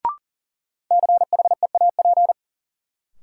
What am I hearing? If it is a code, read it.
Morse code, heard as a keyed tone, about 38 wpm, 700 Hz. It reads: CHEAP
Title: Sentences from Top 1200 Words - New Word Drill - 40wpm